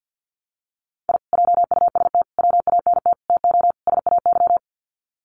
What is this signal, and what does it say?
Morse code, heard as a keyed tone, about 50 wpm, 700 Hz. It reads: I JUST WANT TO SAY
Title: Sets of 5 Words Encore 50wpm